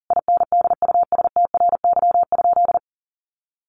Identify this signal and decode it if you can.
Morse code, heard as a keyed tone, about 40 wpm, 700 Hz. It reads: INDUSTRY?
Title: Sentences from Top 1300 Words 40wpm